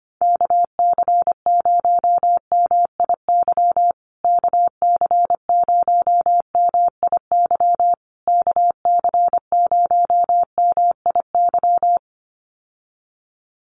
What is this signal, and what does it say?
Morse code, heard as a keyed tone, about 25 wpm, 700 Hz. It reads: KC0MIY KC0MIY KC0MIY
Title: USA Call Signs Repeated 3x 25wpm